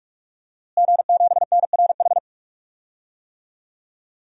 Morse code, heard as a keyed tone, about 46 wpm, 700 Hz. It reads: G7NRH
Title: DXCC-100 Call Signs - 5 Characters - Repeated 3x - 45wpm